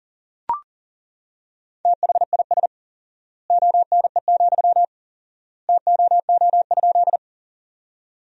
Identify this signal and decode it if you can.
Morse code, heard as a keyed tone, about 40 wpm, 700 Hz. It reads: THIS ONE, TOO?
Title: 4x Word-Spacing - Sentences from Top 200 Words - 40wpm